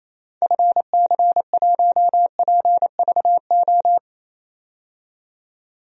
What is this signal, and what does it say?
Morse code, heard as a keyed tone, about 28 wpm, 700 Hz. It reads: FC1PVO
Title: DXCC-100 Call Signs - 6 Characters - 28wpm